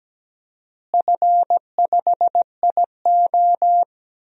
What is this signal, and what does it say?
Morse code, heard as a keyed tone, about 17 wpm, 700 Hz. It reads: F5IO
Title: DXCC-100 Call Signs - 4 Characters - Repeated 3x - 17wpm